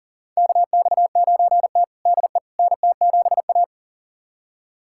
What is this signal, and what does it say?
Morse code, heard as a keyed tone, about 40 wpm, 700 Hz. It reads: KX9T DE NT7A
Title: USA Callsign DE Callsign - 4 Characters - 40wpm